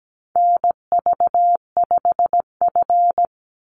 Morse code, heard as a keyed tone, about 17 wpm, 700 Hz. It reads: NV5F
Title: Call Signs - 17wpm